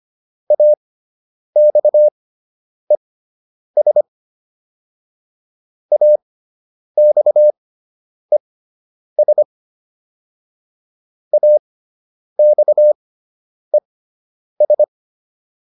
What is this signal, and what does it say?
Morse code, heard as a keyed tone, about 25 wpm, spaced as for 9 wpm, 600 Hz. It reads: AXES AXES AXES
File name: ES4 09